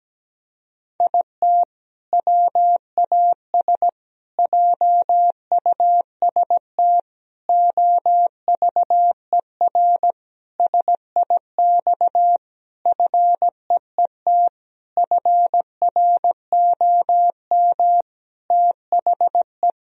Morse code, heard as a keyed tone, about 17 wpm, 700 Hz. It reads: IT WAS JUST OVER SIX FEET FROM THE
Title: Sentences from Top 1000 Words - Review Part 2 - 17wpm